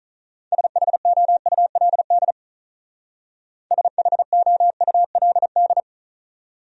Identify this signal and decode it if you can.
Morse code, heard as a keyed tone, about 43 wpm, 700 Hz. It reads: SHOULD SHOULD
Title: Speed-Racing Top 300 Words 35wpm